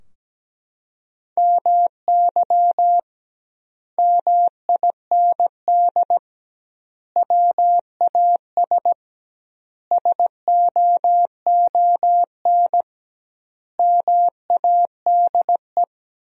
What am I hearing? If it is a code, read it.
Morse code, heard as a keyed tone, about 17 wpm, 700 Hz. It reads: MY MIND WAS SOON MADE
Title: 2x Word-Spacing - Sentences from Top 500 Words - 17wpm